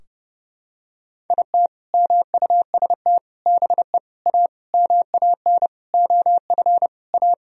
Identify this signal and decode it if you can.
Morse code, heard as a keyed tone, about 32 wpm, 700 Hz. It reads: IT MUST BE A MAN OF A
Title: Sentences from Top 1000 Words - Review Part 3 - 30wpm